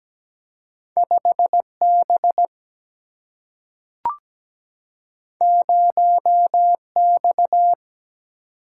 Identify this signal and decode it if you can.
Morse code, heard as a keyed tone, about 17 wpm, 700 Hz. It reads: 5 B 0 X
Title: Single-Letter Single-Number 17wpm